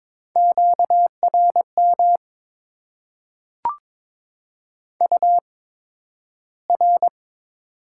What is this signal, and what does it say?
Morse code, heard as a keyed tone, about 22 wpm, spaced as for 12 wpm, 700 Hz. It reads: QRM UR
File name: Top 90 QSO Elements - 8x Character Spacing - 22wpm